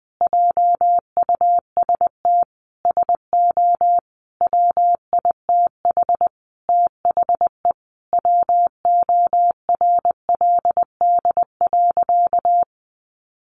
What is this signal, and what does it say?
Morse code, heard as a keyed tone, about 19 wpm, 700 Hz. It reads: JUST SO WITH THE WORLD.
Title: Sentences from Top 200 Words 20wpm